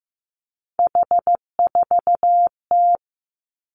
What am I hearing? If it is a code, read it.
Morse code, heard as a keyed tone, about 15 wpm, 700 Hz. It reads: H4T
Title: Three Challenging Letter-Number Combinations 15wpm